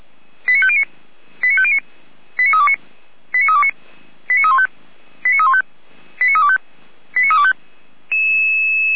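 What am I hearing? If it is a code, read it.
Five-tone selective call (ZVEI), heard as quick runs of five tones, tones of about 80 ms. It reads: 89509 89509 89329 89329 89315 89315 89325 89325
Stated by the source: ZVEI five-tone selective call 89333 received via SDR.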